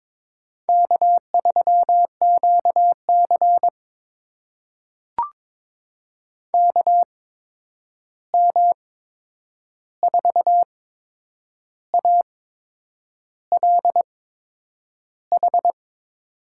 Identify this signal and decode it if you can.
Morse code, heard as a keyed tone, about 22 wpm, spaced as for 8 wpm, 700 Hz. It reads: K3QC KM4ALH
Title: USA Call Signs - 8x Character Spacing - 22wpm